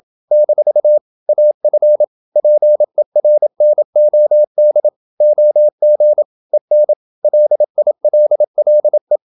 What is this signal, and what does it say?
Morse code, heard as a keyed tone, about 27 wpm, 600 Hz. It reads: - AF PERNOD OG EN LILLE
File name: Ferie QRQ 27wpm